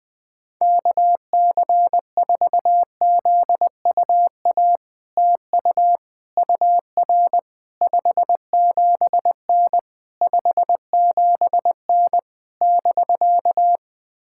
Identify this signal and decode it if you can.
Morse code, heard as a keyed tone, about 20 wpm, 700 Hz. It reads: KC4ZUA TU UR 57N 57N <BK>
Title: POTA QSO Practice - 20wpm